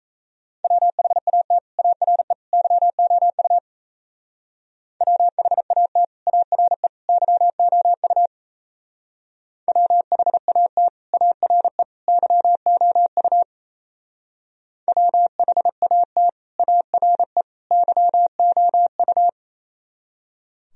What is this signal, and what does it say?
Morse code, heard as a keyed tone, about 32 wpm, 700 Hz. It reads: WHAT ARE YOU WHAT ARE YOU WHAT ARE YOU WHAT ARE YOU
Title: Speed-Racing - Sets of 3 Words 28wpm